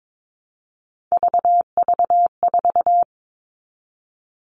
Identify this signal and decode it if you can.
Morse code, heard as a keyed tone, about 22 wpm, 700 Hz. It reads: VV4
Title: All Cut Numbers - 3-Digit 22wpm